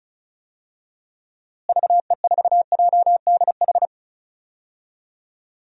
Morse code, heard as a keyed tone, about 34 wpm, 700 Hz. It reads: VE4JDH
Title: DXCC-100 Call Signs - 6 Characters - Repeated 2x - 35wpm